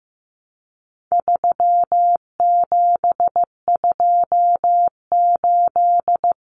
Morse code, heard as a keyed tone, about 15 wpm, 700 Hz. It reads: 3728
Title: Common Cut Numbers - 4-Digit 15wpm